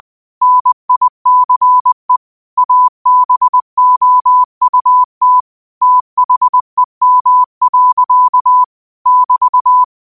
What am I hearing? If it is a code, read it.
Morse code, heard as a keyed tone, about 20 wpm, 1000 Hz. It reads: NICE ABOUT THEM. <BT>
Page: Quotes for Fri, 15 Aug 2025 in Morse Code at 20 words per minute.